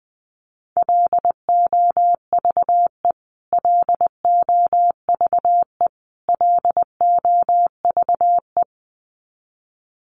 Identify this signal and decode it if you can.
Morse code, heard as a keyed tone, about 20 wpm, 700 Hz. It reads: LOVE LOVE LOVE
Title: Top 500 Words Repeated 20wpm